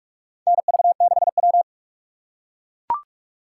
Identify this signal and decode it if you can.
Morse code, heard as a keyed tone, about 43 wpm, 700 Hz. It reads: NV6W
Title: USA Call Signs - 4 Characters - 6x Character Spacing - 45wpm